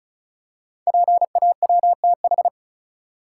Morse code, heard as a keyed tone, about 36 wpm, 700 Hz. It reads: PAWTH
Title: Code Groups - 5 Letters - 35wpm